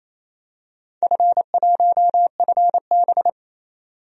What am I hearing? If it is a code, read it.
Morse code, heard as a keyed tone, about 28 wpm, 700 Hz. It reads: F1FB
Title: DXCC-100 Call Signs - 4 Characters - 28wpm